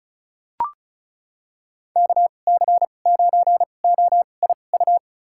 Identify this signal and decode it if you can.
Morse code, heard as a keyed tone, about 36 wpm, spaced as for 25 wpm, 700 Hz. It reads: KC9OIU
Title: USA Call Signs - 6 Characters - 2x Character Spacing - 35wpm